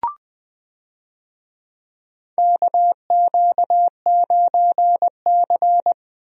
Morse code, heard as a keyed tone, about 20 wpm, 700 Hz. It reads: KQ9C
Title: USA Call Signs - 4 Characters - 20wpm